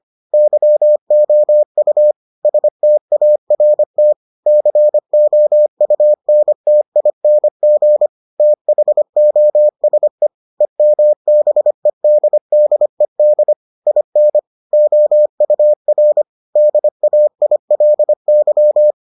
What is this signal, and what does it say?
Morse code, heard as a keyed tone, about 25 wpm, 600 Hz. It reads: YOU START COUNTING THOSE EMBEDDED IN OUR DAILY